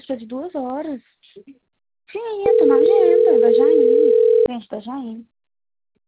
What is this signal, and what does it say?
Ringback tone (North America), 440 + 480 Hz, with speech.
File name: out- - - - - .52491